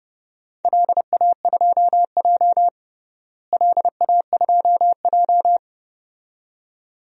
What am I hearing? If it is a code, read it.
Morse code, heard as a keyed tone, about 30 wpm, 700 Hz. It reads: LA2J LA2J
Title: DXCC-100 Call Signs - 4 Characters - Repeated 2x - 30wpm